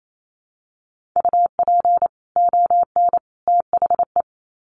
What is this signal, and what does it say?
Morse code, heard as a keyed tone, about 28 wpm, 700 Hz. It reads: UP ON THE
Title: Sets of 3 Words 28wpm